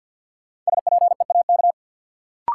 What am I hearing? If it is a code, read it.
Morse code, heard as a keyed tone, about 47 wpm, 700 Hz. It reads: SPEAK
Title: Sentences from Top 1000 Words - Review Part 2 - New Word Drill - 50wpm